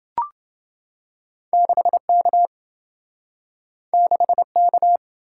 Morse code, heard as a keyed tone, about 27 wpm, 700 Hz. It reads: 6K 6K